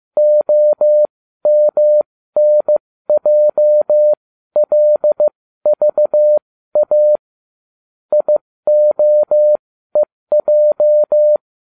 Morse code, heard as a keyed tone, about 15 wpm, spaced as for 13 wpm, 600 Hz. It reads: OMNJLVA IOEJ